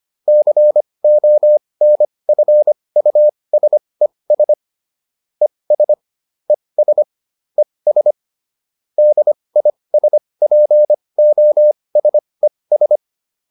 Morse code, heard as a keyed tone, about 25 wpm, spaced as for 20 wpm, 600 Hz. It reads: CONFUSES ES ES ES DISPOSES
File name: ES8 20